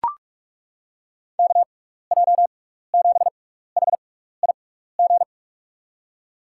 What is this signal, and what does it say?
Morse code, heard as a keyed tone, about 41 wpm, 700 Hz. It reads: K J 7 H I G
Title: USA Call Signs - 6 Characters - 6x Character Spacing - 45wpm